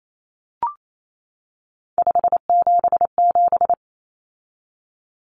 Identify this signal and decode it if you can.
Morse code, heard as a keyed tone, about 28 wpm, 700 Hz. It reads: 577
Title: Numbers - 3-Digit 28wpm